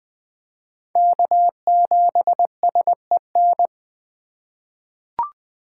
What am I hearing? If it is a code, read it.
Morse code, heard as a keyed tone, about 20 wpm, 700 Hz. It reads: K7SEN
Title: 2023 POTA Hunter Call Signs - 20wpm